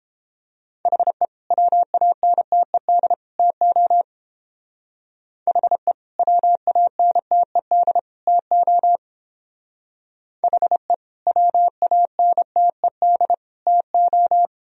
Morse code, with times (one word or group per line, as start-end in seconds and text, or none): 0.9-1.2 HE
1.5-3.1 WANTED
3.4-4.0 TO
5.5-5.9 HE
6.2-8.0 WANTED
8.3-9.0 TO
10.4-10.9 HE
11.3-13.3 WANTED
13.7-14.5 TO